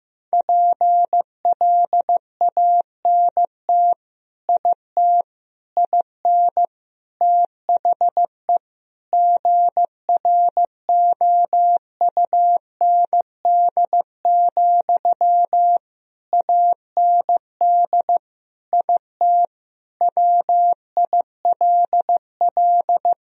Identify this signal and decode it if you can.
Morse code, heard as a keyed tone, about 15 wpm, 700 Hz. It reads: PLANT IT IN THE GROUND, AND IT WILL
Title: Sentences from Top 1000 Words - Review Part 2 - 15wpm